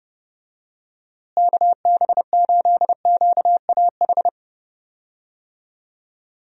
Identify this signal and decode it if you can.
Morse code, heard as a keyed tone, about 30 wpm, 700 Hz. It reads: KB8QAH
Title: USA Call Signs - 6 Characters - Repeated 2x - 30wpm